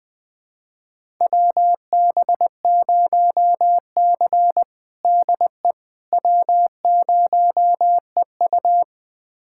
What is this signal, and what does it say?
Morse code, heard as a keyed tone, about 20 wpm, 700 Hz. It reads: WB0C DE W0EU
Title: USA Callsign DE Callsign - 4 Characters - 20wpm